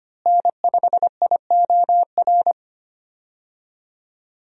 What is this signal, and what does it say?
Morse code, heard as a keyed tone, about 25 wpm, 700 Hz. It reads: N5IOR
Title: USA Call Signs - 5 Characters - Repeated 3x - 25wpm